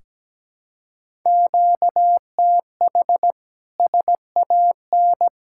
Morse code, heard as a keyed tone, about 17 wpm, 700 Hz. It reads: QTH SAN
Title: K1USN SST QSO Practice 17wpm